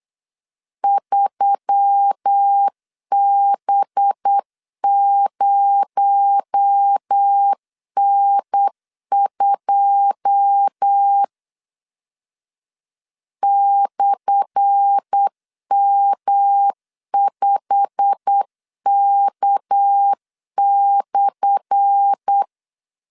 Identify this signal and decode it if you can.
Morse code, heard as a keyed tone, about 8 wpm, 800 Hz. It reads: 3B0N2 /M5K/
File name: LEZIONI / 10 SIGNS / 10A SLASH 8wpm 3452